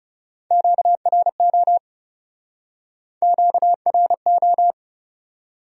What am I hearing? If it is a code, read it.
Morse code, heard as a keyed tone, about 35 wpm, 700 Hz. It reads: QRO QRO
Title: Speed-Racing Q-Codes - 30wpm